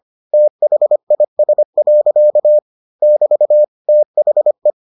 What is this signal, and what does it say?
Morse code, heard as a keyed tone, about 25 wpm, 600 Hz. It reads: THIS. <BT> THE